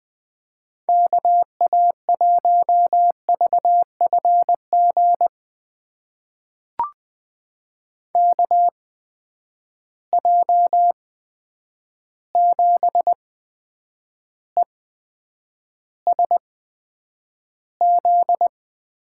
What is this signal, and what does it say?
Morse code, heard as a keyed tone, about 20 wpm, spaced as for 9 wpm, 700 Hz. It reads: KA1VFG KJ7ESZ
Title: USA Call Signs - 6 Characters - 8x Character Spacing - 20wpm